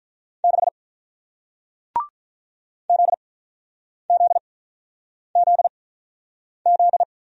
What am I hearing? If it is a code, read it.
Morse code, heard as a keyed tone, about 48 wpm, spaced as for 9 wpm, 700 Hz. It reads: 6 ZZZZ